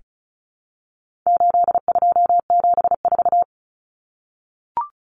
Morse code, heard as a keyed tone, about 34 wpm, 700 Hz. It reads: 8274
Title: Numbers - 4-Digit 35wpm